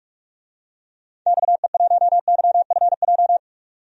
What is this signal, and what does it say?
Morse code, heard as a keyed tone, about 46 wpm, 700 Hz. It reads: XE1YFJ
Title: DXCC-100 Call Signs - 6 Characters - 45wpm